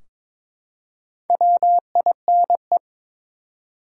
Morse code, heard as a keyed tone, about 23 wpm, 700 Hz. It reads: WINE
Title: Four Letter Words - 22wpm